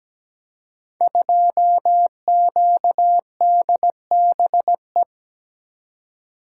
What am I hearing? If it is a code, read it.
Morse code, heard as a keyed tone, about 17 wpm, 700 Hz. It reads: 2QDBE
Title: Code Groups - 5 Letter:Numbers - 17wpm